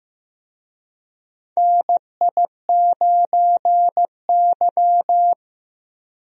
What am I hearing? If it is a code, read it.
Morse code, heard as a keyed tone, about 15 wpm, 700 Hz. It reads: NI9Y
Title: USA Call Signs - 4 Characters - Repeated 3x - 15wpm